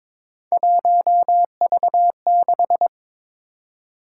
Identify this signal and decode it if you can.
Morse code, heard as a keyed tone, about 22 wpm, 700 Hz. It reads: 1V6
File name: practice / 22wpm / Three Letter-Number Combinations - 22wpm